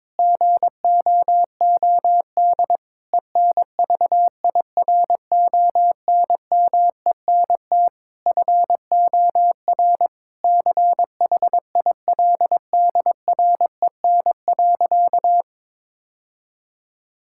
Morse code, heard as a keyed tone, about 22 wpm, 700 Hz. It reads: GOOD ENVIRONMENT FOR CHILDREN.
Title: Sentences from Top 1000 Words - Review Part 5 - 22wpm